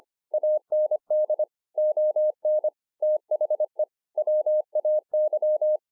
Morse code, heard as a keyed tone, about 25 wpm, 610 Hz. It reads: AND ON THE WAY
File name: 2011-04-mp3 / THIS IS NOT MY SHACK 2 25WPM